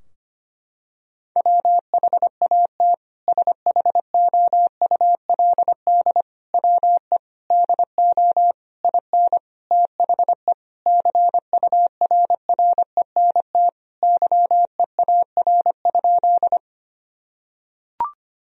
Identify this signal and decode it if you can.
Morse code, heard as a keyed tone, about 25 wpm, 700 Hz. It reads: WHAT SHOULD WE DO IN THE CURRENT YEAR?
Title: Sentences from Top 1000 Words - Review Part 4 - 25wpm